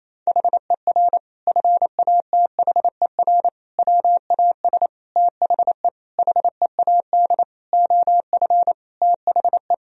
Morse code, heard as a keyed tone, about 28 wpm, 700 Hz. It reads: HER FATHER WAS THE HEAD OF THE
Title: Sentences from Top 1000 Words - Review Part 4 - 28wpm